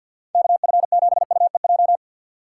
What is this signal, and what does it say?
Morse code, heard as a keyed tone, about 53 wpm, 700 Hz. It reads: KF7UEJ
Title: USA Call Signs - 6 Characters - 3x Character Spacing - 50wpm